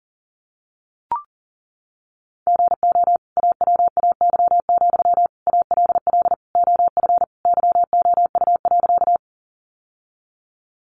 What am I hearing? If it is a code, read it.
Morse code, heard as a keyed tone, about 37 wpm, 700 Hz. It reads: GO AWAY, ALL OF YOU.
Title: Sentences from Top 600 Words 40wpm